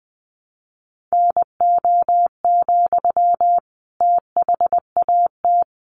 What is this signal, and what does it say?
Morse code, heard as a keyed tone, about 20 wpm, 700 Hz. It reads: NO, THAT
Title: Sentences from Top 600 Words 20wpm